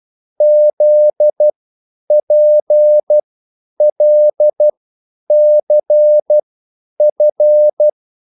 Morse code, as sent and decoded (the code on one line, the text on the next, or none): --.. .--. .-.. -.-. ..-.
ZPLCF